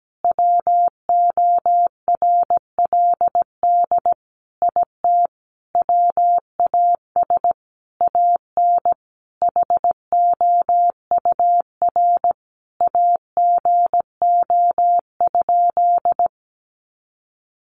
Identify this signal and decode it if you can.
Morse code, heard as a keyed tone, about 17 wpm, 700 Hz. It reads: WORLD IT WAS AN HOUR AGO?
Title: Sentences from Top 1300 Words 17wpm